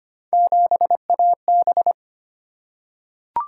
Morse code, heard as a keyed tone, about 25 wpm, 700 Hz. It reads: 7AB
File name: Three Letter-Number Combinations - 25wpm